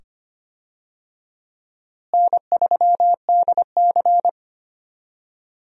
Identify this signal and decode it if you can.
Morse code, heard as a keyed tone, about 25 wpm, 700 Hz. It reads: N3DC
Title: USA Call Signs - 4 Characters - Repeated 2x - 25wpm